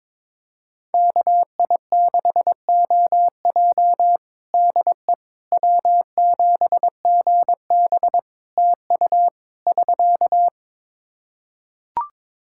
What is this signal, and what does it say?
Morse code, heard as a keyed tone, about 22 wpm, 700 Hz. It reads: KI6OJ DE W7GB TU <SK>
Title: Practice QSO - Exchange 4 - Ending - 22wpm